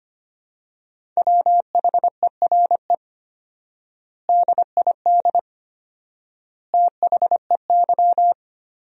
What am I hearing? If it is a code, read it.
Morse code, heard as a keyed tone, about 25 wpm, 700 Hz. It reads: WHERE DID THEY
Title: 4x Word-Spacing - Sentences from Top 400 Words - 25wpm